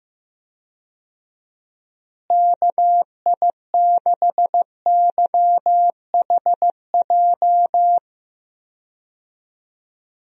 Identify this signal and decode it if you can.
Morse code, heard as a keyed tone, about 15 wpm, 700 Hz. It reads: KI6YHJ
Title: USA Call Signs - 6 Characters - Repeated 2x - 15wpm